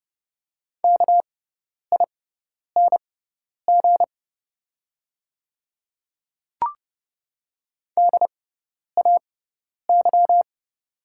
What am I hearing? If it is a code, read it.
Morse code, heard as a keyed tone, about 30 wpm, spaced as for 10 wpm, 700 Hz. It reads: KING DAY
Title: 6x Character Spacing - Top 500 Words - 30wpm